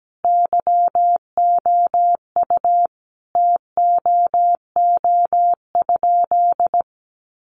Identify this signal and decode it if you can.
Morse code, heard as a keyed tone, about 17 wpm, 700 Hz. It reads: YOU TOO?
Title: Sentences from Top 1500 Words 17wpm